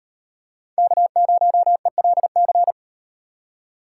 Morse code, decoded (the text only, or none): K0ELC